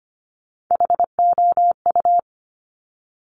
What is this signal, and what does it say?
Morse code, heard as a keyed tone, about 25 wpm, 700 Hz. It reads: HOU
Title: Top Three Letter Combinations in Words 25wpm